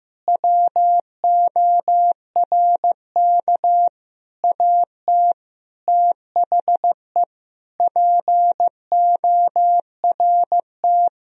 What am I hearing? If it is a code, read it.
Morse code, heard as a keyed tone, about 16 wpm, 700 Hz. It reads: WORK AT THE PORT
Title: Sentences from Top 1000 Words - Review Part 1 - 15wpm